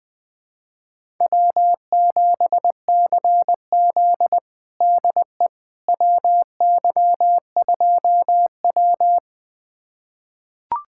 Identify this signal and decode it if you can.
Morse code, heard as a keyed tone, about 20 wpm, 700 Hz. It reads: W7CZ DE WY2W
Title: USA Callsign DE Callsign - 4 Characters - 20wpm